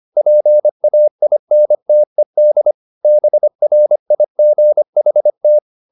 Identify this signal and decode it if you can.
Morse code, heard as a keyed tone, about 25 wpm, 600 Hz. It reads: PAINTED BRIGHT